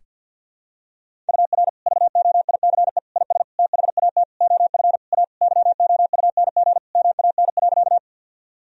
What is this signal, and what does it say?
Morse code, heard as a keyed tone, about 48 wpm, 700 Hz. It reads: UR VOICE IS THAT OF A YOUNG MAN.
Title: Sentences from Top 500 Words and Abbreviations 50wpm